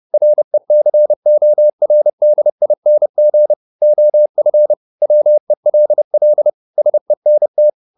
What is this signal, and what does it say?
Morse code, heard as a keyed tone, about 30 wpm, 600 Hz. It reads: RECORDING OF WELL SENT